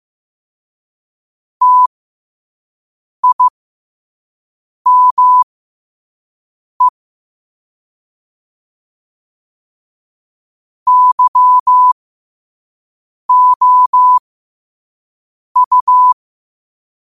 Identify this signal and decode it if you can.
Morse code, heard as a keyed tone, about 15 wpm, spaced as for 5 wpm, 1000 Hz. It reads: TIME YOU
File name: 5wpm